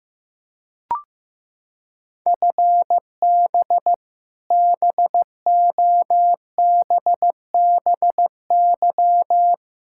Morse code, heard as a keyed tone, about 16 wpm, 700 Hz. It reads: FB BOBBY
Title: Practice QSO - Exchange 3 - More Info - 15wpm